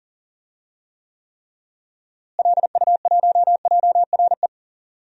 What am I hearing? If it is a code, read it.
Morse code, heard as a keyed tone, about 38 wpm, 700 Hz. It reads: LU1JRE
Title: DXCC-100 Call Signs - 6 Characters - Repeated 3x - 40wpm